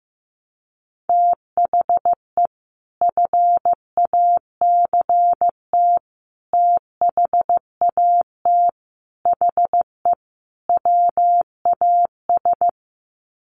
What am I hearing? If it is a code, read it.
Morse code, heard as a keyed tone, about 15 wpm, 700 Hz. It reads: THE FACT THAT HE WAS
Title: Sets of 5 Words 15wpm